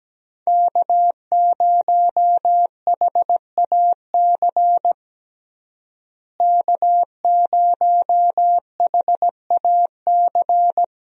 Morse code, heard as a keyed tone, about 17 wpm, 700 Hz. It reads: K0HAC K0HAC
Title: USA Call Signs - 5 Characters - Repeated 3x - 17wpm